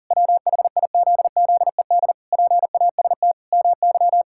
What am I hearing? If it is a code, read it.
Morse code, heard as a keyed tone, about 40 wpm, 700 Hz. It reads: WHIZZED PAST MY
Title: Pangram Challenge v2 - 40wpm